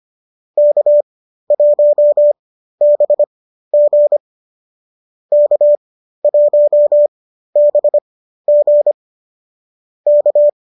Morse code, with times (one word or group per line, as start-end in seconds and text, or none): 0.6-4.2 K1BG
5.3-8.9 K1BG
10.1-10.5 K